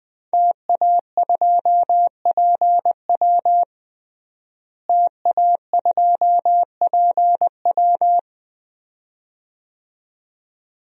Morse code, heard as a keyed tone, about 21 wpm, 700 Hz. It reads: TA2PW TA2PW
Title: DXCC-100 Call Signs - 5 Characters - Repeated 2x - 20wpm